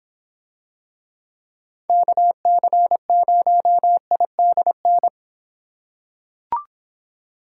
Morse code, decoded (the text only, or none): KC0IDN